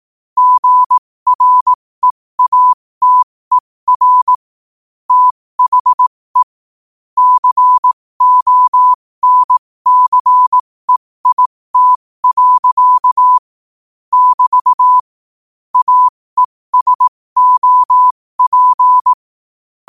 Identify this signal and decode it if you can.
Morse code, heard as a keyed tone, about 18 wpm, spaced as for 15 wpm, 1000 Hz. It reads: GREATER THE CONCEIT. <BT> AESOP
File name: qotdfiles / 15wpm